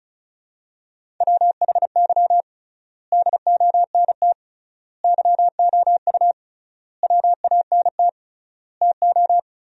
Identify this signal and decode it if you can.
Morse code, heard as a keyed tone, about 34 wpm, 700 Hz. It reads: WHY DONT YOU WANT TO
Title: 3x Word-Spacing - Sentences from Top 300 Words - 35wpm